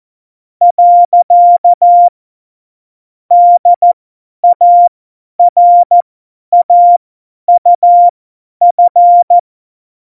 Morse code, heard as a keyed tone, about 14 wpm, spaced as for 10 wpm, 700 Hz. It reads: . DARAUF
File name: Morsekurs / Witze / Witz 03.Dezember 2019, 50 F70